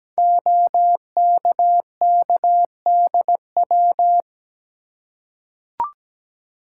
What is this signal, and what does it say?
Morse code, heard as a keyed tone, about 17 wpm, 700 Hz. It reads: OKKDW
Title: Code Groups - 5 Letter:Numbers - 17wpm